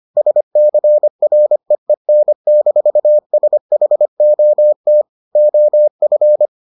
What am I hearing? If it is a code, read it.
Morse code, heard as a keyed tone, about 25 wpm, 600 Hz. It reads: SCREEN-SHOT OF